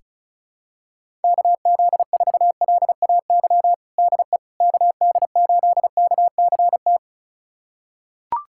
Morse code, heard as a keyed tone, about 36 wpm, 700 Hz. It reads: KZ4LAY DE KD8KCT
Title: USA Callsign DE Callsign - 6 Characters - 35wpm